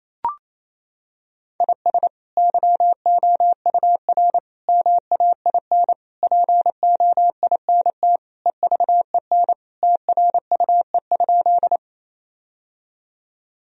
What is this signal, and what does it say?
Morse code, heard as a keyed tone, about 28 wpm, 700 Hz. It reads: IS YOUR MAIN POINT EVEN TRUE?
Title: Sentences from Top 1000 Words - Review Part 2 - 28wpm